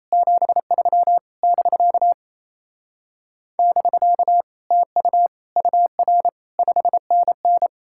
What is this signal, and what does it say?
Morse code, heard as a keyed tone, about 37 wpm, 700 Hz. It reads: 73 <BK> <BK> TU UR 5NN